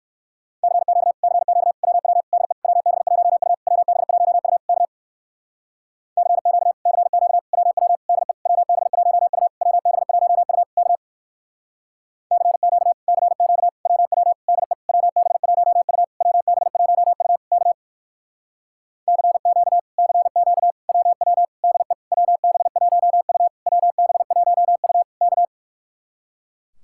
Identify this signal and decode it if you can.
Morse code, heard as a keyed tone, about 58 wpm, 700 Hz. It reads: CQ CQ WW DE WB1U WB1U K CQ CQ WW DE WB1U WB1U K CQ CQ WW DE WB1U WB1U K CQ CQ WW DE WB1U WB1U K